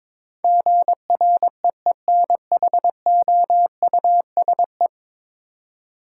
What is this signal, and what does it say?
Morse code, heard as a keyed tone, about 22 wpm, 700 Hz. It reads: GREENHOUSE
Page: morse-code-ninja-practice-22wpm